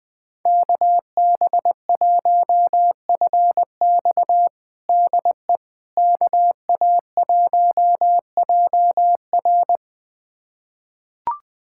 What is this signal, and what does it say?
Morse code, heard as a keyed tone, about 21 wpm, 700 Hz. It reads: KB1FX DE KA1JR
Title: USA Callsign DE Callsign - 5 Characters - 20wpm